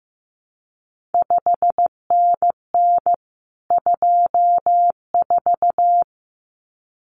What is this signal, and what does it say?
Morse code, heard as a keyed tone, about 15 wpm, 700 Hz. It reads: 5NN 24
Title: CQ WW DX Contest Exchange Encore - 15wpm